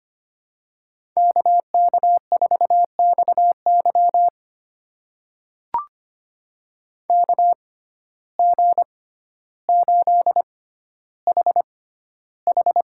Morse code, heard as a keyed tone, about 25 wpm, spaced as for 13 wpm, 700 Hz. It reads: KK4XY KG8HH
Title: USA Call Signs - 5 Characters - 6x Character Spacing - 25wpm